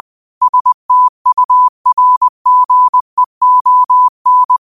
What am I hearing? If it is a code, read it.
Morse code, heard as a keyed tone, about 20 wpm, 1000 Hz. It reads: STURGEON